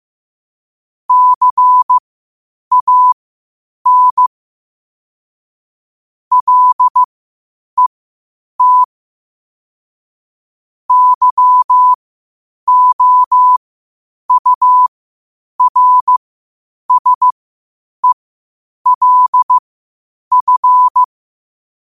Morse code, heard as a keyed tone, about 15 wpm, spaced as for 8 wpm, 1000 Hz. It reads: CAN LET YOURSELF
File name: qotdfiles / 8wpm